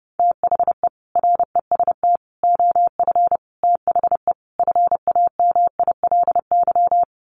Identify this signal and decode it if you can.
Morse code, heard as a keyed tone, about 30 wpm, 700 Hz. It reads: THE REST OF THE FAMILY
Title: Sets of 5 Words Encore2 30wpm